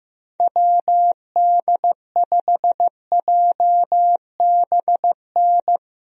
Morse code, heard as a keyed tone, about 15 wpm, 700 Hz. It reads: WD5JBN
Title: USA Call Signs - 6 Characters - Repeated 2x - 15wpm